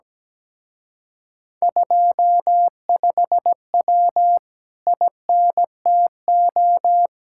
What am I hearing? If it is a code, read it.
Morse code, heard as a keyed tone, about 17 wpm, 700 Hz. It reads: 25W INTO
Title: Speed-Racing Common QSO Phrases Encore - 17wpm